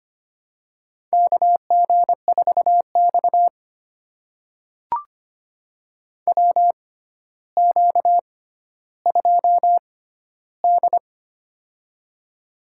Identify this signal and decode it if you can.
Morse code, heard as a keyed tone, about 25 wpm, spaced as for 13 wpm, 700 Hz. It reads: KG4X WQ2D
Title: USA Call Signs - 4 Characters - 6x Character Spacing - 25wpm